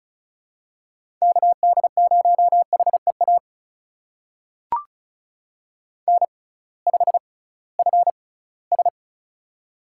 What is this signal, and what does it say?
Morse code, heard as a keyed tone, about 36 wpm, spaced as for 21 wpm, 700 Hz. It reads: KD0HEA N5FS
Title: USA Call Signs - 6x Character Spacing - 35wpm